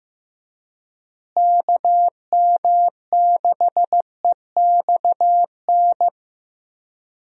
Morse code, heard as a keyed tone, about 15 wpm, 700 Hz. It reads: KM6EXN
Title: USA Call Signs - 6 Characters - Repeated 2x - 15wpm